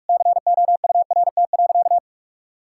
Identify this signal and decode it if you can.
Morse code, heard as a keyed tone, about 41 wpm, 700 Hz. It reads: COURT.
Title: Sentences from Top 1600 Words 45wpm